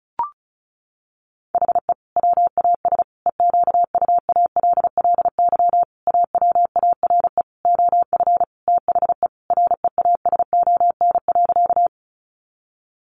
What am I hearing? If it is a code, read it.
Morse code, heard as a keyed tone, about 34 wpm, 700 Hz. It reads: HE WAS EQUALLY AWARE OF THE REASON.
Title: Sentences from Top 1400 Words 35wpm